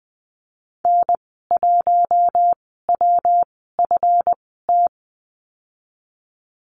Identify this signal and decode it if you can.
Morse code, heard as a keyed tone, about 20 wpm, spaced as for 15 wpm, 700 Hz. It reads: N1WFT
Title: 179 Lesson 29g - Slash - Call Signs 20wpm